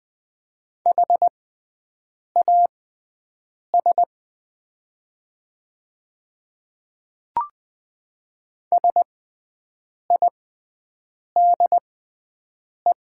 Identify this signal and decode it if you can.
Morse code, heard as a keyed tone, about 20 wpm, spaced as for 7 wpm, 700 Hz. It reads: HAS SIDE
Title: 6x Character Spacing - Top 300 Words - 20wpm